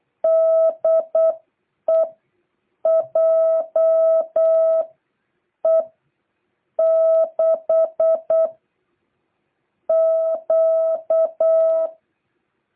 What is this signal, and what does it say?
Morse code, heard as a keyed tone, about 8 wpm, spaced as for 6 wpm, 660 Hz. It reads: DE JE6Q